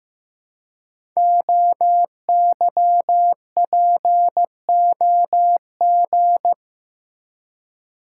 Morse code, heard as a keyed tone, about 15 wpm, 700 Hz. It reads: OYPOG
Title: Code Groups - 5 Letters - 15wpm